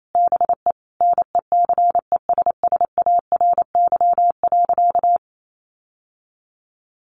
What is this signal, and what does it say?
Morse code, heard as a keyed tone, about 28 wpm, 700 Hz. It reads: BE NECESSARY.
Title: Sentences from Top 800 Words 28wpm